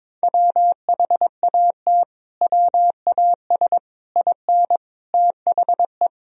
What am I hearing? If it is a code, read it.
Morse code, heard as a keyed tone, about 22 wpm, 700 Hz. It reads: WHAT WAS IN THE
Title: Sentences from Top 1000 Words - Review Part 2 - 22wpm